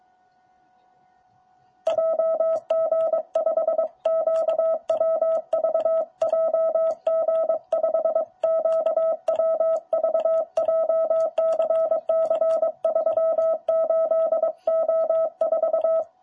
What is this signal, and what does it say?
Morse code, heard as a keyed tone, about 23 wpm, 650 Hz. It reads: JG5QWV JG5QWV JCC38O4
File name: jh8cbh / 2021cw